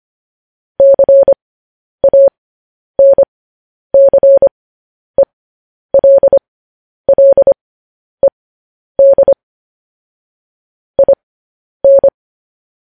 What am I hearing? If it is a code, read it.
Morse code, heard as a keyed tone, about 25 wpm, spaced as for 10 wpm, 550 Hz. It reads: CANCELLED IN
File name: podcast / News headlines-1x-25x10